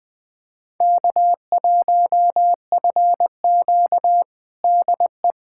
Morse code, heard as a keyed tone, about 21 wpm, 700 Hz. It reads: K1FQ DE
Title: USA Callsign DE Callsign - 4 Characters - 20wpm